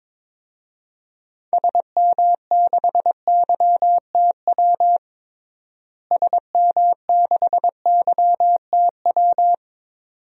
Morse code, heard as a keyed tone, about 22 wpm, 700 Hz. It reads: SM6YTW SM6YTW
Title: DXCC-100 Call Signs - 6 Characters - Repeated 2x - 22wpm